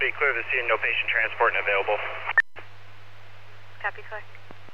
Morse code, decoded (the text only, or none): none